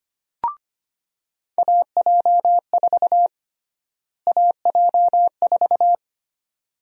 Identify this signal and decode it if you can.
Morse code, heard as a keyed tone, about 25 wpm, 700 Hz. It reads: AJ4 AJ4
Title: USA Call Sign Prefixes - Repeated 2x - 25wpm